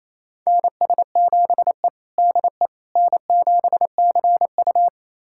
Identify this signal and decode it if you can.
Morse code, heard as a keyed tone, about 28 wpm, 700 Hz. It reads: NS7E DE N7CU
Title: USA Callsign DE Callsign - 4 Characters - 28wpm